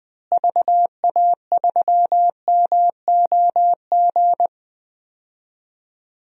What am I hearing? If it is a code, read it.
Morse code, heard as a keyed tone, about 20 wpm, 700 Hz. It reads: VA3MOG
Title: DXCC-100 Call Signs - 6 Characters - Repeated 2x - 20wpm